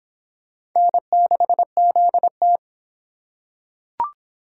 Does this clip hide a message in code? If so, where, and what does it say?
Yes: Morse code from 0.8 to 2.6 s, heard as a keyed tone, about 23 wpm, 700 Hz. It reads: N6ZT